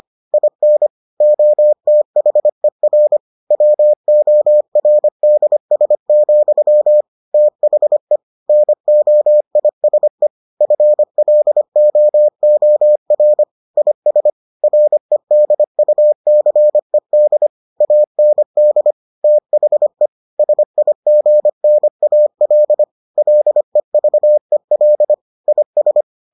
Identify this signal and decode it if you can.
Morse code, heard as a keyed tone, about 25 wpm, 600 Hz. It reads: IN OTHER WORDS, THE NOISE FLOOR IS REDUCED AND THE SIGNAL LEVEL IS